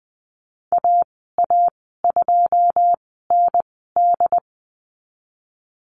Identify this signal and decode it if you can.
Morse code, heard as a keyed tone, about 20 wpm, spaced as for 15 wpm, 700 Hz. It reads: AA2ND
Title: 128 Lesson 21g - Number 6 - Call Signs 20wpm